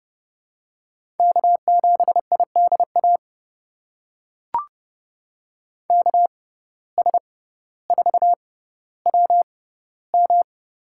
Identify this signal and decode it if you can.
Morse code, heard as a keyed tone, about 30 wpm, spaced as for 15 wpm, 700 Hz. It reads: K7IDA KS4WM
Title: USA Call Signs - 5 Characters - 6x Character Spacing - 30wpm